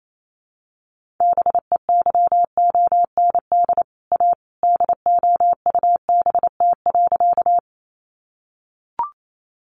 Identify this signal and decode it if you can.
Morse code, heard as a keyed tone, about 28 wpm, 700 Hz. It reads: BEYOND A DOUBT.
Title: Sentences from Top 1500 Words 28wpm